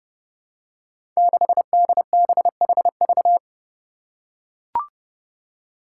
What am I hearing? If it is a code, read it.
Morse code, heard as a keyed tone, about 29 wpm, 700 Hz. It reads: 6DBHV
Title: Five Challenging Letter-Number Combinations 30wpm